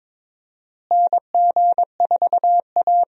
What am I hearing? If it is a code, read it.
Morse code, heard as a keyed tone, about 22 wpm, 700 Hz. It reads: NG4A
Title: USA Call Signs - 4 Characters - Repeated 3x - 22wpm